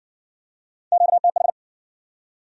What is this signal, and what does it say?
Morse code, heard as a keyed tone, about 56 wpm, 700 Hz. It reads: QTH